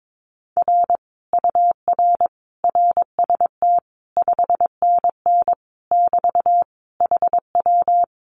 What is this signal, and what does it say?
Morse code, heard as a keyed tone, about 22 wpm, 700 Hz. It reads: R UR RST 5NN <BT> HW
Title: Common QSO Phrases Encore 22wpm